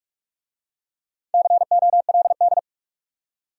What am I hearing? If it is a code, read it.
Morse code, heard as a keyed tone, about 44 wpm, 700 Hz. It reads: COLD